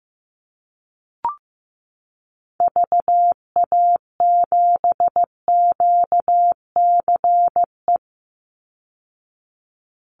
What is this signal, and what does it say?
Morse code, heard as a keyed tone, about 15 wpm, 700 Hz. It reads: VA7QCE
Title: Call Signs Repeated 3x - 15wpm